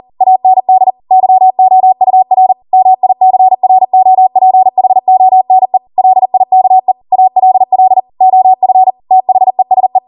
Morse code, heard as a keyed tone, about 40 wpm, 750 Hz. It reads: AND YOUR MICROPHONE LIKE ALL OF THESE